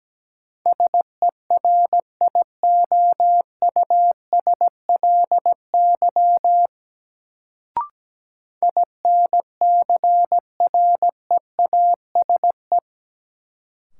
Morse code, heard as a keyed tone, about 17 wpm, 700 Hz. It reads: SERIOUSLY INCREASE
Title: Sentences from Top 1200 Words - New Word Drill Alt - 17wpm